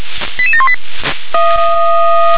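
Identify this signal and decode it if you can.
Five-tone selective call (ZVEI), heard as quick runs of five tones, tones of about 70 ms. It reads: 88718
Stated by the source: ZVEI five-tone selective call received with an SDR.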